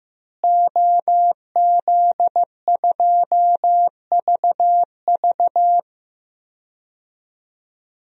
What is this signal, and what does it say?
Morse code, heard as a keyed tone, about 15 wpm, 700 Hz. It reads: OZ2VV
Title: DXCC-100 Call Signs - 5 Characters - Repeated 2x - 15wpm